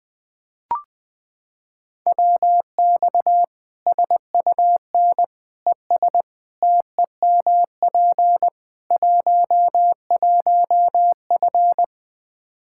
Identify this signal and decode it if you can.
Morse code, heard as a keyed tone, about 20 wpm, 700 Hz. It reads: WX SUN ES TEMP 11F
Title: Practice QSO - Exchange 2 - Info - 20wpm